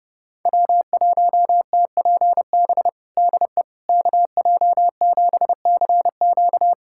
Morse code, heard as a keyed tone, about 30 wpm, 700 Hz. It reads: W1TPB DE KJ7CQ
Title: USA Callsign DE Callsign - 5 Characters - 30wpm